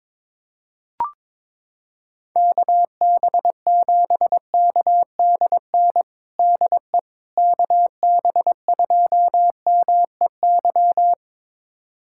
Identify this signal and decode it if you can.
Morse code, heard as a keyed tone, about 22 wpm, 700 Hz. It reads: KB7KDN DE KB2MEY
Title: USA Callsign DE Callsign - 6 Characters - 22wpm